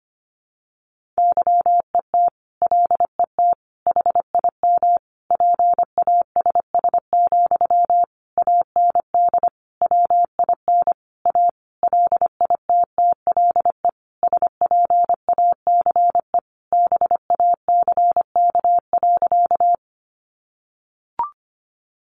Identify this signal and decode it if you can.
Morse code, heard as a keyed tone, about 25 wpm, 700 Hz. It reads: YET LET HIM PASS, AND WIN A LITTLE SPACE BACK.
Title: Sentences from Top 1500 Words 25wpm